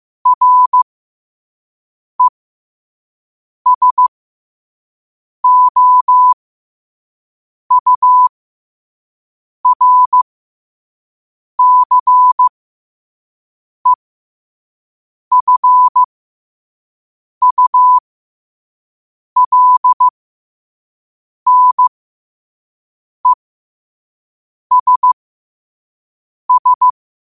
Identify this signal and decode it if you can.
Morse code, heard as a keyed tone, about 15 wpm, spaced as for 5 wpm, 1000 Hz. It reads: RESOURCEFULNESS
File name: qotdfiles / 5wpm